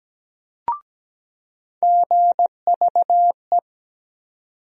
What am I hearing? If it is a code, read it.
Morse code, heard as a keyed tone, about 17 wpm, 700 Hz. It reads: GVE
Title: USA Call Sign Postfixes - 3 Letters 17wpm